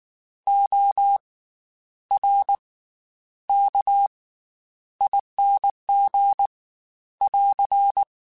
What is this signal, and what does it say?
Morse code, heard as a keyed tone, about 19 wpm, 800 Hz. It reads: O R K ING <AR>
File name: Special INGs / Special INGs-mp3s / Slow ing @18